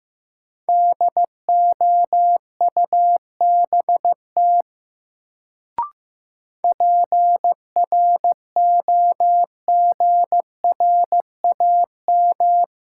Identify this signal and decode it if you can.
Morse code, heard as a keyed tone, about 15 wpm, 700 Hz. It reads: DOUBT PROGRAM
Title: Sentences from Top 1500 Words - New Word Drill Alt - 15wpm